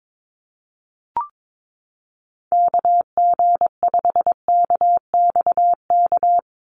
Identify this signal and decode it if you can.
Morse code, heard as a keyed tone, about 22 wpm, 700 Hz. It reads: KG5KXK
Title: USA Call Signs 22wpm